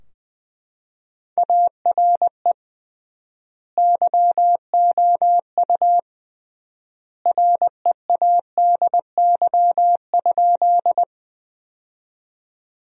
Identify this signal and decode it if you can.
Morse code, heard as a keyed tone, about 20 wpm, 700 Hz. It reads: ARE YOU READY?
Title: 3x Word-Spacing - Sentences from Top 400 Words - 20wpm